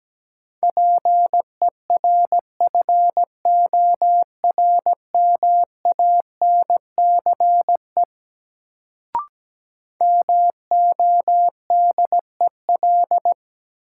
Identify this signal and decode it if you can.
Morse code, heard as a keyed tone, about 17 wpm, 700 Hz. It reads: PERFORMANCE MODEL
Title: Sentences from Top 1000 Words - New Word Drill - 17wpm